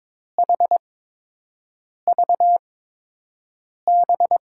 Morse code, heard as a keyed tone, about 22 wpm, spaced as for 6 wpm, 700 Hz. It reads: HVB
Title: USA Call Signs - 5 Characters - 8x Character Spacing - 22wpm